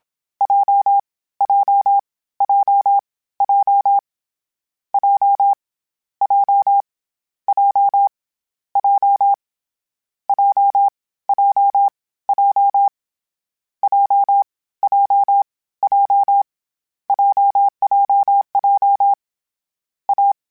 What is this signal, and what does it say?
Morse code, heard as a keyed tone, about 27 wpm, 800 Hz. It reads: J J J J J J J J J J J J J J J J J A
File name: Difficult Combos / Difficult Combos-mp3s / AWJ @25